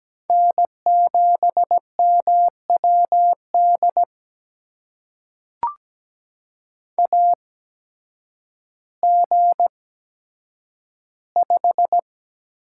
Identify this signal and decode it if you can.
Morse code, heard as a keyed tone, about 17 wpm, spaced as for 9 wpm, 700 Hz. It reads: N7MWD AG5
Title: USA Call Signs - 5 Characters - 8x Character Spacing - 17wpm